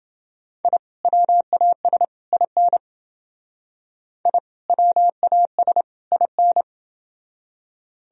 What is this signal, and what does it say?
Morse code, heard as a keyed tone, about 31 wpm, 700 Hz. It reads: I WAS IN I WAS IN